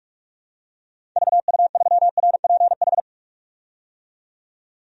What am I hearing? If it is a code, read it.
Morse code, heard as a keyed tone, about 46 wpm, 700 Hz. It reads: VU3RPH
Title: DXCC-100 Call Signs - 6 Characters - 45wpm